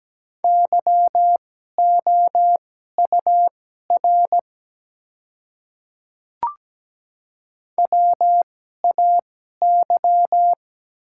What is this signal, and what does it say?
Morse code, heard as a keyed tone, about 17 wpm, spaced as for 12 wpm, 700 Hz. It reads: YOUR WAY
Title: 2x Character Spacing - Top 100 Words - 17wpm